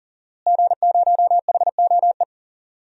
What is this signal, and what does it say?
Morse code, heard as a keyed tone, about 40 wpm, 700 Hz. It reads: G0HOE
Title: DXCC-100 Call Signs - 5 Characters - Repeated 3x - 40wpm